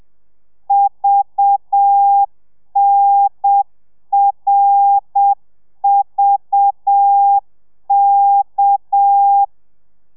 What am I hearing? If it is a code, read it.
Morse code, heard as a keyed tone, about 7 wpm, 800 Hz. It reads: VNRVK